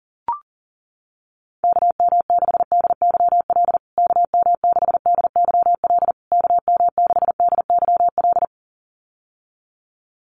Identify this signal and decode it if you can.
Morse code, heard as a keyed tone, about 40 wpm, 700 Hz. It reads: KM6DYL KM6DYL KM6DYL
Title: USA Call Signs Repeated 3x 40wpm